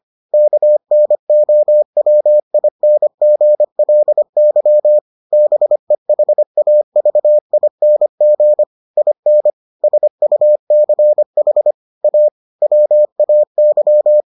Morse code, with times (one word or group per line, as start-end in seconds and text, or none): 0.3-5.0 KNOWINGLY
5.3-8.6 BEHAVING
9.0-9.5 IN
9.8-11.7 SUCH
12.0-12.3 A
12.6-14.2 WAY